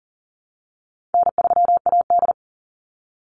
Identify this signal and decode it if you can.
Morse code, heard as a keyed tone, about 40 wpm, 700 Hz. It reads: N3AD
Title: Call Signs Repeated 2x - 40wpm